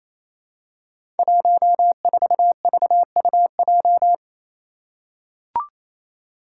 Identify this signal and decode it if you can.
Morse code, heard as a keyed tone, about 28 wpm, 700 Hz. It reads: 14VUJ
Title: Five Challenging Letter-Number Combinations 28wpm